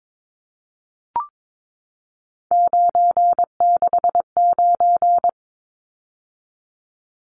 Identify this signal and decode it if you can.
Morse code, heard as a keyed tone, about 22 wpm, 700 Hz. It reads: 969
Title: Numbers - 3-Digit 22wpm